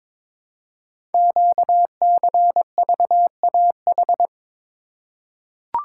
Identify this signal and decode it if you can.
Morse code, heard as a keyed tone, about 22 wpm, 700 Hz. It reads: QCVAH
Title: Code Groups - 5 Letters - 22wpm